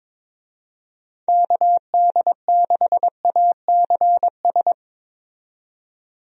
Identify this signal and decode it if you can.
Morse code, heard as a keyed tone, about 22 wpm, 700 Hz. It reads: KD6ACS
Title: USA Call Signs - 6 Characters - 22wpm